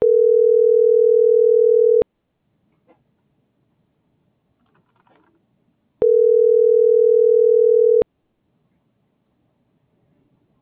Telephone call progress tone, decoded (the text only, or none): ringback tone